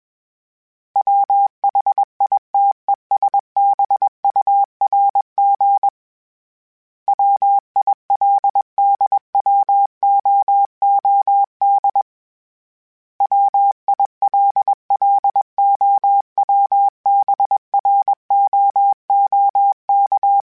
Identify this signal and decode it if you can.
Morse code, heard as a keyed tone, about 21 wpm, 800 Hz. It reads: WHITESBURG WILDWOOD WILLOWBROOK
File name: fromSMs / FromSMs mp3s / Cities@20